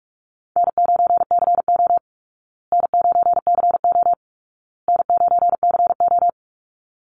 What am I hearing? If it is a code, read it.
Morse code, heard as a keyed tone, about 44 wpm, 700 Hz. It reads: N9CO N9CO N9CO
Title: Easy Call Signs Repeated 3x - 45wpm